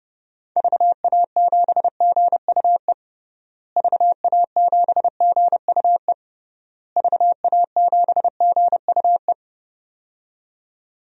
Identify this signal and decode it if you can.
Morse code, heard as a keyed tone, about 32 wpm, 700 Hz. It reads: VA7GUE VA7GUE VA7GUE
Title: DXCC-100 Call Signs - 6 Characters - Repeated 3x - 30wpm